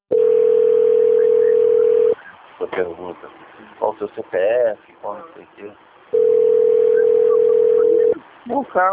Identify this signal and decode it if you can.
Ringback tone (North America), 440 + 480 Hz, with speech.
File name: out- - - - - .9108